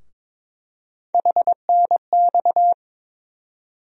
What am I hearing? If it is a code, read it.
Morse code, heard as a keyed tone, about 22 wpm, 700 Hz. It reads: HNX
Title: USA Call Sign Postfixes - 3 Letters 22wpm